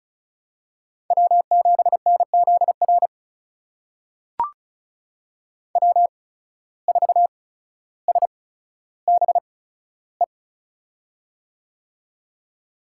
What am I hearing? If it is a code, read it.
Morse code, heard as a keyed tone, about 36 wpm, spaced as for 15 wpm, 700 Hz. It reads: W7NZR W4SBE
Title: USA Call Signs - 5 Characters - 8x Character Spacing - 35wpm